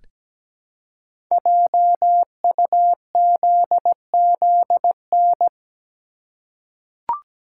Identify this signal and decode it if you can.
Morse code, heard as a keyed tone, about 17 wpm, 700 Hz. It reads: JUZZN
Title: Code Groups - 5 Letters - 17wpm